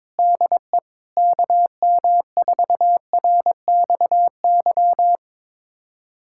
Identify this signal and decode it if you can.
Morse code, heard as a keyed tone, about 22 wpm, 700 Hz. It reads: DE KM4RXY
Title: USA Callsign DE Callsign 22wpm